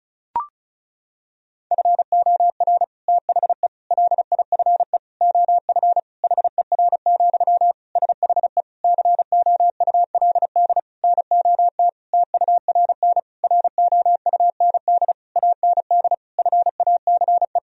Morse code, heard as a keyed tone, about 36 wpm, 700 Hz. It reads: FOR THE LIFE OF HER, SHE COULD NOT TURN ROUND AND FACE
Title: Sentences from Top 1000 Words - Review Part 2 - 35wpm